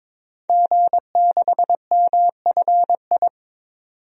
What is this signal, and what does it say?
Morse code, heard as a keyed tone, about 22 wpm, 700 Hz. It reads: G6MFI
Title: DXCC-100 Call Signs - 5 Characters - 22wpm